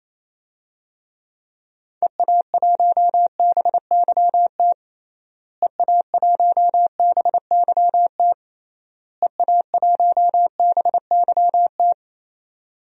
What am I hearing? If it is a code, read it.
Morse code, heard as a keyed tone, about 28 wpm, 700 Hz. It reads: EA1BYT EA1BYT EA1BYT
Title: DXCC-100 Call Signs - 6 Characters - Repeated 3x - 28wpm